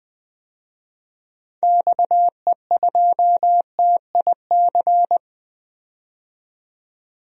Morse code, heard as a keyed tone, about 20 wpm, 700 Hz. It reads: XE2TIC
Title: DXCC-100 Call Signs - 6 Characters - 20wpm